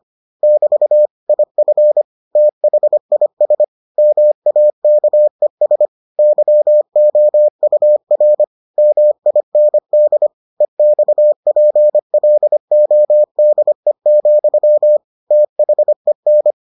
Morse code, heard as a keyed tone, about 25 wpm, 600 Hz. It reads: <BT> IF THIS MAKES YOUR MIND EXPLODE, THEN